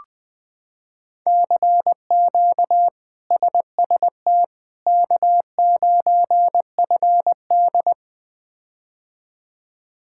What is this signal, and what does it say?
Morse code, heard as a keyed tone, about 21 wpm, 700 Hz. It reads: CQ SST K9FD
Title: K1USN SST QSO Practice - 20wpm